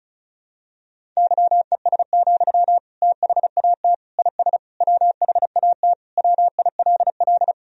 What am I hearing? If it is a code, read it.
Morse code, heard as a keyed tone, about 36 wpm, 700 Hz. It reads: YES, THAT IS WHAT WILL
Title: Sentences from Top 500 Words and Abbreviations 35wpm